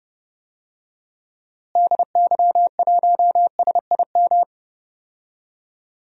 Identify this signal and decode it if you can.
Morse code, heard as a keyed tone, about 29 wpm, 700 Hz. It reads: DY1SIM
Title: DXCC-100 Call Signs - 6 Characters - Repeated 3x - 30wpm